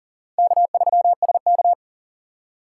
Morse code, heard as a keyed tone, about 40 wpm, 700 Hz. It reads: K3SK
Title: USA Call Signs - 4 Characters - 4x Character Spacing - 40wpm